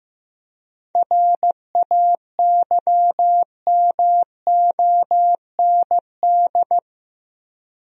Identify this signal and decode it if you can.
Morse code, heard as a keyed tone, about 15 wpm, 700 Hz. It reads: RAYMOND
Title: USA Common Names - 15wpm